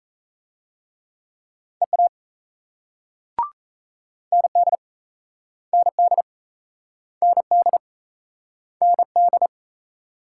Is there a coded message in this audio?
Yes: Morse code from 1.8 to 9.5 s, heard as a keyed tone, about 38 wpm, 700 Hz. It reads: EA ND ND ND ND